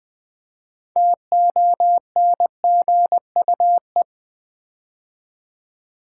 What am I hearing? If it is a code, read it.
Morse code, heard as a keyed tone, about 20 wpm, 700 Hz. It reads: TONGUE
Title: 6 Letter Words - First 3 Letters Given - 20wpm